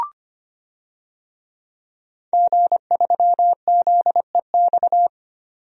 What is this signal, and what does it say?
Morse code, heard as a keyed tone, about 25 wpm, 700 Hz. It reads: G3ZEX
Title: DXCC-100 Call Signs - 5 Characters - 25wpm